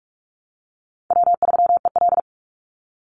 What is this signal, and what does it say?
Morse code, heard as a keyed tone, about 44 wpm, 700 Hz. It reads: W3EL
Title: Call Signs Repeated 2x - 45wpm